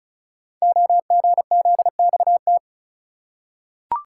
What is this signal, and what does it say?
Morse code, heard as a keyed tone, about 36 wpm, 700 Hz. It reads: OGZXT
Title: Code Groups - 5 Letters - 35wpm